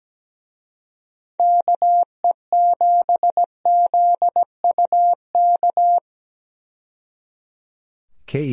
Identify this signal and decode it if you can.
Morse code, heard as a keyed tone, about 17 wpm, 700 Hz. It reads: KE7ZUK
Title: USA Call Signs - 6 Characters - 17wpm